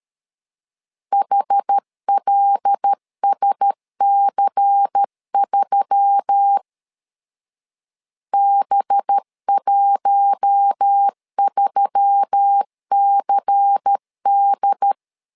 Morse code, heard as a keyed tone, about 13 wpm, 800 Hz. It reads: HLSC3 B13CD
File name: LEZIONI / 5 LBVC / 5 LBVC 12wpm 1955